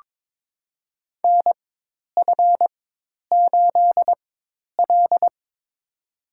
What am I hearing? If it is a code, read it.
Morse code, heard as a keyed tone, about 22 wpm, spaced as for 10 wpm, 700 Hz. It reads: NF8L
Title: USA Call Signs - 4 Characters - 4x Character Spacing - 22wpm